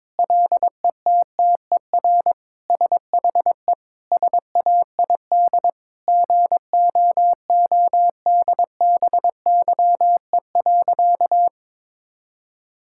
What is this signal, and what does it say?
Morse code, heard as a keyed tone, about 22 wpm, 700 Hz. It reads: LETTER SHE SAID GOODBYE.
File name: practice / 22wpm / Sentences with 4-Word Phrases - 22wpm